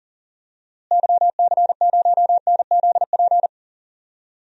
Morse code, heard as a keyed tone, about 39 wpm, 700 Hz. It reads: YC0NZP
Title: DXCC-100 Call Signs - 6 Characters - Repeated 2x - 40wpm